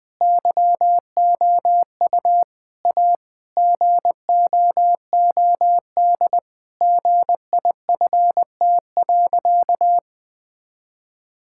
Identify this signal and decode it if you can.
Morse code, heard as a keyed tone, about 19 wpm, 700 Hz. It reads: YOU A GOOD GIFT.
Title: Sentences from Top 1000 Words - Review Part 6 - 20wpm